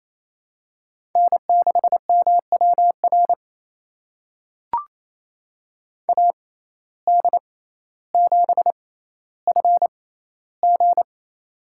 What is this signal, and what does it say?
Morse code, heard as a keyed tone, about 28 wpm, spaced as for 14 wpm, 700 Hz. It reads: N6MWR AD7FG
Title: USA Call Signs - 5 Characters - 6x Character Spacing - 28wpm